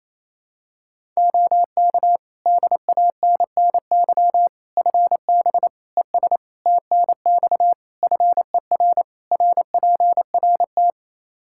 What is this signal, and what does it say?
Morse code, heard as a keyed tone, about 28 wpm, 700 Hz. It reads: OK DANNY FB ES TNX FER RPRT
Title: Practice QSO - Exchange 2 - Info - 28wpm